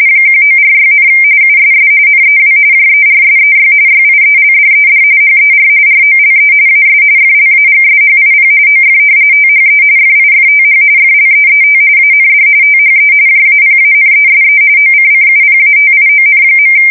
RTTY, about 45.45 baud. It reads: 1946, USS NEW JERSEY YIELDED HER ROLE AS FLAGSHIP OF THE FIFTH FLEET AND THE JAPANESE OCCUPATION.
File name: fsk170 / battleshipnj